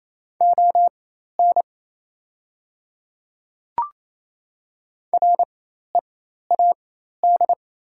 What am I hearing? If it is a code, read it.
Morse code, heard as a keyed tone, about 27 wpm, spaced as for 13 wpm, 690 Hz. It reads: ON READ
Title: 4x Character Spacing - Top 200 Words - 28wpm